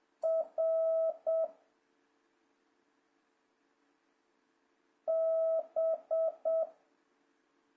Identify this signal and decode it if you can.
Morse code, heard as a keyed tone, about 7 wpm, 660 Hz. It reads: R B
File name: jh8cbh / 2026cw